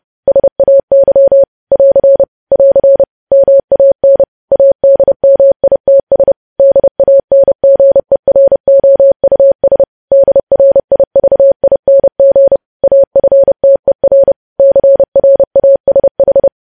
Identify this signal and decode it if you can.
Morse code, heard as a keyed tone, about 30 wpm, 550 Hz. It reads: SAY <AR> <AR> MAN ADMITS DANGEROUS DRIVING AFTER CRASH
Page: news headlines in morse code at 30 wpm